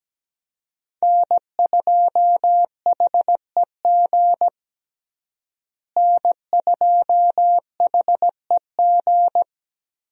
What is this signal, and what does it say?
Morse code, heard as a keyed tone, about 17 wpm, 700 Hz. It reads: N2HEG N2HEG
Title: USA Call Signs - 5 Characters - Repeated 3x - 17wpm